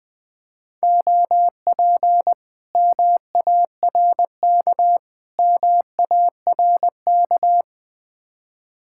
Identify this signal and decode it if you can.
Morse code, heard as a keyed tone, about 20 wpm, 700 Hz. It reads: OP MARK MARK
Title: Speed-Racing Common QSO Phrases Encore - 17wpm